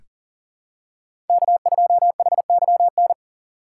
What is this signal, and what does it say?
Morse code, heard as a keyed tone, about 39 wpm, 700 Hz. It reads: K2HYN
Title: USA Call Signs - 5 Characters - 3x Character Spacing - 40wpm